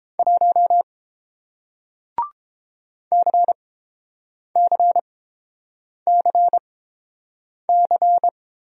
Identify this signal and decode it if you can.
Morse code, heard as a keyed tone, about 30 wpm, spaced as for 7 wpm, 700 Hz. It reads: 1 CCCC